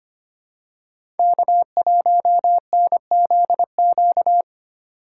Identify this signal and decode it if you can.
Morse code, heard as a keyed tone, about 25 wpm, 700 Hz. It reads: K1NZQ
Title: USA Call Signs - 3x Character Spacing - 25wpm